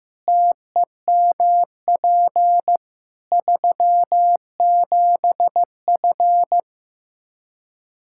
Morse code, heard as a keyed tone, about 15 wpm, 700 Hz. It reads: TEMP 37F
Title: Practice QSO - Exchange 2 - Info - 15wpm